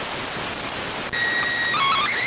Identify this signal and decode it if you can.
SSTV (slow-scan television), PD 290.